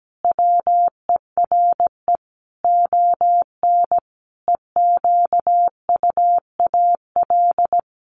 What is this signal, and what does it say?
Morse code, heard as a keyed tone, about 17 wpm, 700 Hz. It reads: WERE ON EQUAL
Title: Sentences from Top 1400 Words 17wpm